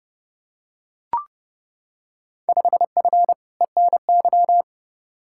Morse code, heard as a keyed tone, about 30 wpm, 700 Hz. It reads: 5F ENY
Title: Field Day Exchanges Encore - 30wpm